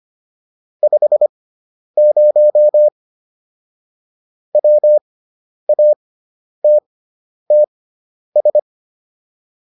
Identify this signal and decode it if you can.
Morse code, heard as a keyed tone, about 25 wpm, spaced as for 10 wpm, 600 Hz. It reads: 50 WATTS